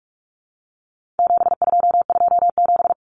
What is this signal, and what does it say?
Morse code, heard as a keyed tone, about 44 wpm, 700 Hz. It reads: 7227
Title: Common Cut Numbers - 4-Digit 45wpm